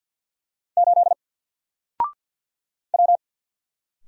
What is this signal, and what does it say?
Morse code, heard as a keyed tone, about 50 wpm, 700 Hz. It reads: 8 W
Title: Single Letter-Number Alt 50wpm